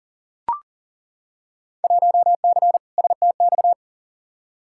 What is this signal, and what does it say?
Morse code, heard as a keyed tone, about 39 wpm, 700 Hz. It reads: 1C STX
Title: Field Day Exchanges Encore - 40wpm